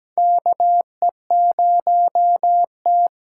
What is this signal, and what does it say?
Morse code, heard as a keyed tone, about 17 wpm, 700 Hz. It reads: KE0T
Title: USA Call Signs - 6 Characters - 8x Character Spacing - 17wpm